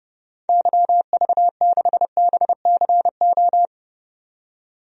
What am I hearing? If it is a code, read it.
Morse code, heard as a keyed tone, about 31 wpm, 700 Hz. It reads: YV6BCO
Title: DXCC-100 Call Signs - 6 Characters - 30wpm